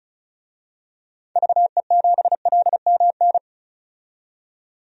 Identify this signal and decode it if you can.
Morse code, heard as a keyed tone, about 36 wpm, 700 Hz. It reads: VE7LMN
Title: DXCC-100 Call Signs - 6 Characters - 35wpm